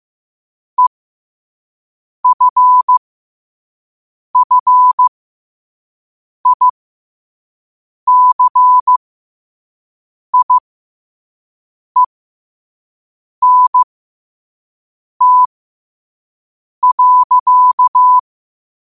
Morse code, heard as a keyed tone, about 15 wpm, spaced as for 5 wpm, 1000 Hz. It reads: EFFICIENT.
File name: qotdfiles / 5wpm